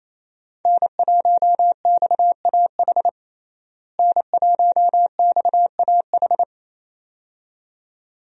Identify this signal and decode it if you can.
Morse code, heard as a keyed tone, about 28 wpm, 700 Hz. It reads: N1XAH N1XAH
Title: USA Call Signs - 5 Characters - Repeated 2x - 28wpm